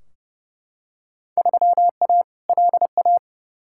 Morse code, heard as a keyed tone, about 29 wpm, 700 Hz. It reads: 3A LA
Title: Field Day Exchanges - 30wpm